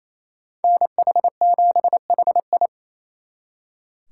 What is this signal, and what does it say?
Morse code, heard as a keyed tone, about 36 wpm, 700 Hz. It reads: NH7HI